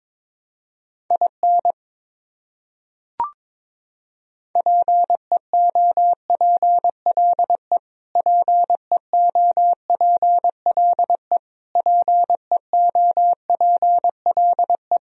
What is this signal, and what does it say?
Morse code, heard as a keyed tone, about 22 wpm, 700 Hz. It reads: IN PEOPLE PEOPLE PEOPLE
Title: Top 100 Words Repeated 22wpm